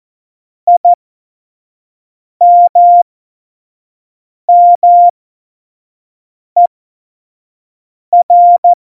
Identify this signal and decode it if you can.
Morse code, heard as a keyed tone, about 14 wpm, spaced as for 5 wpm, 700 Hz. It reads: IMMER
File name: Morsekurs / Witze / Witz 05. Dezember 2017, 25 F70